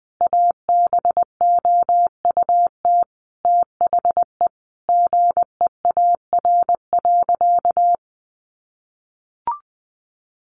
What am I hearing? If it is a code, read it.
Morse code, heard as a keyed tone, about 19 wpm, 700 Hz. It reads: ABOUT THE GEAR.
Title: Sentences from Top 1400 Words 20wpm